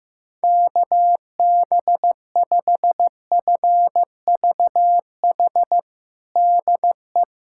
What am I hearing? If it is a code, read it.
Morse code, heard as a keyed tone, about 15 wpm, 700 Hz. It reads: KB5FVH DE
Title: USA Callsign DE Callsign 15wpm